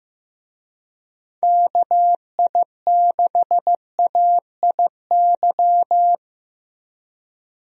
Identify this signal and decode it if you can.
Morse code, heard as a keyed tone, about 15 wpm, 700 Hz. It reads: KI6AIY
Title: USA Call Signs - 6 Characters - Repeated 3x - 15wpm